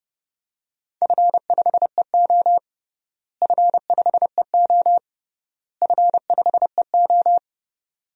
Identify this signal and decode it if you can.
Morse code, heard as a keyed tone, about 30 wpm, 700 Hz. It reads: F5EO F5EO F5EO
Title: DXCC-100 Call Signs - 4 Characters - Repeated 3x - 30wpm